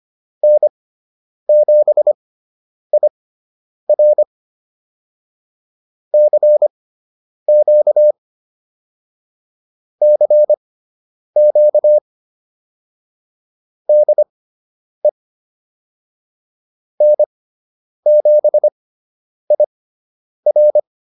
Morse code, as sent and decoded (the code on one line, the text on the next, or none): -. --... .. .-. / -.-. --.- / -.-. --.- / -.. / -. --... .. .-.
N7IR CQ CQ DE N7IR